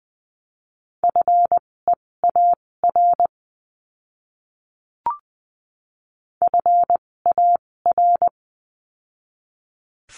Morse code, heard as a keyed tone, about 20 wpm, spaced as for 16 wpm, 700 Hz. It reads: FEAR FAR
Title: 256 Lesson 42a - Words 12-20wpm